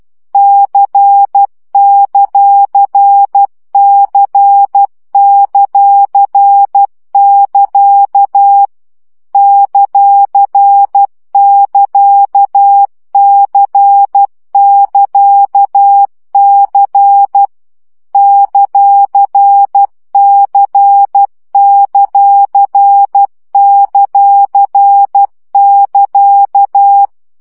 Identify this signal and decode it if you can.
Morse code, heard as a keyed tone, about 12 wpm, 800 Hz. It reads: C;C;<KA> ;<KA>C<KA>C ;C;;<KA>
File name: CW gyakorlas c ct pontosvesszo 10 baud